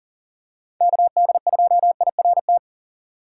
Morse code, heard as a keyed tone, about 40 wpm, 700 Hz. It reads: KD2IRT
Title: USA Call Signs - 4x Character Spacing - 40wpm